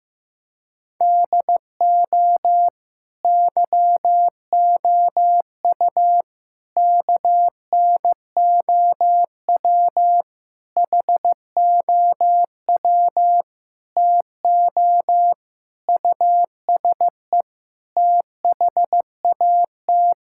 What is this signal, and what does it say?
Morse code, heard as a keyed tone, about 15 wpm, 700 Hz. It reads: DO YOU KNOW HOW TO USE THAT
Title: Sentences from Top 1000 Words - Review Part 6 - 15wpm